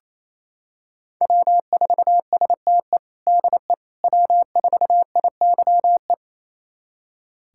Morse code, heard as a keyed tone, about 28 wpm, 700 Hz. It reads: W4STE DE W4IYE
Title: USA Callsign DE Callsign - 5 Characters - 28wpm